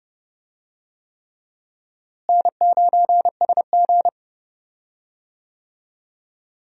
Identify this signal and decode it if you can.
Morse code, heard as a keyed tone, about 30 wpm, 700 Hz. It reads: N9SG
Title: USA Call Signs - 4 Characters - Repeated 3x - 30wpm